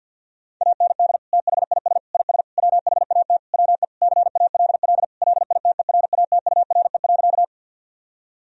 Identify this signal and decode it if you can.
Morse code, heard as a keyed tone, about 51 wpm, 700 Hz. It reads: AND THIS IS WHAT WE CALL LITERATURE.
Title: Sentences from Top 1000 Words - Review Part 3 - 50wpm